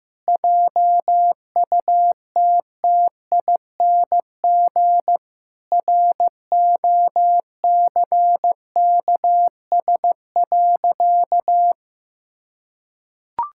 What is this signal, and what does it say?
Morse code, heard as a keyed tone, about 15 wpm, 700 Hz. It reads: JUTTING ROCKS.
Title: Pangram Challenge v2 - 15wpm